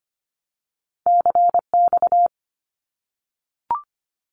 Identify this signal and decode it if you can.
Morse code, heard as a keyed tone, about 25 wpm, 700 Hz. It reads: CX
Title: Abbreviations Repeated 3x - 25wpm